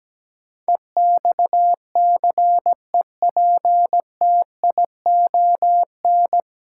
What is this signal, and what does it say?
Morse code, heard as a keyed tone, about 17 wpm, 700 Hz. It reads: EXCEPTION
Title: Sets of 4 Words - New Word Drill - 17wpm